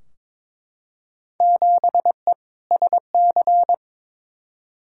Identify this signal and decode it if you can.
Morse code, heard as a keyed tone, about 22 wpm, 700 Hz. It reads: 7E SC
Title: Field Day Exchanges Encore - 22wpm